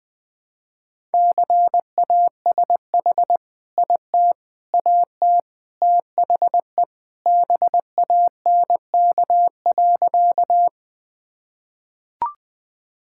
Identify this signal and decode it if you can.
Morse code, heard as a keyed tone, about 20 wpm, 700 Hz. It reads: CASH IT AT THE BANK.
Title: Sentences from Top 1000 Words - Review Part 5 - 20wpm